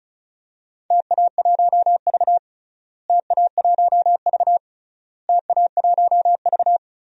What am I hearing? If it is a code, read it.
Morse code, heard as a keyed tone, about 36 wpm, 700 Hz. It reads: TA1V TA1V TA1V
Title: DXCC-100 Call Signs - 4 Characters - Repeated 3x - 35wpm